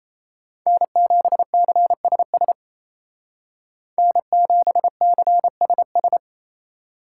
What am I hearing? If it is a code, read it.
Morse code, heard as a keyed tone, about 36 wpm, 700 Hz. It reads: N7CSS N7CSS